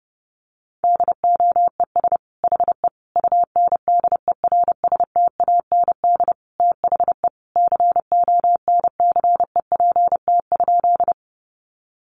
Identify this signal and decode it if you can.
Morse code, heard as a keyed tone, about 30 wpm, 700 Hz. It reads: DOES HE UNDERSTAND THE CONCEPT?
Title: Sentences from Top 1100 Words 30wpm